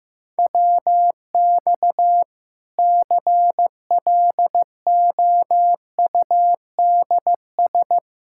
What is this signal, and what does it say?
Morse code, heard as a keyed tone, about 16 wpm, 700 Hz. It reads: WX CLOUDS
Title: Practice QSO - Exchange 2 - Info - 15wpm